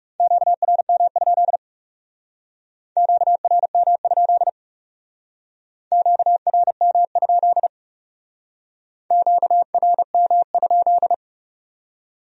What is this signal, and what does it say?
Morse code, heard as a keyed tone, about 37 wpm, 700 Hz. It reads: QRM? QRM? QRM? QRM?
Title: Speed-Racing Q-Codes - 30wpm